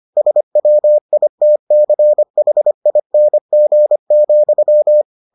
Morse code, heard as a keyed tone, about 25 wpm, 600 Hz. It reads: SWITCHING,